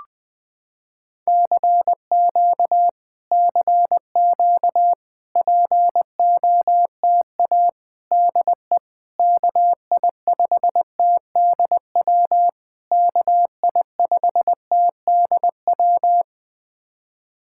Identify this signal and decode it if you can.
Morse code, heard as a keyed tone, about 20 wpm, 700 Hz. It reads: CQ CQ POTA DE KI5TDW KI5TDW
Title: POTA QSO Practice - 20wpm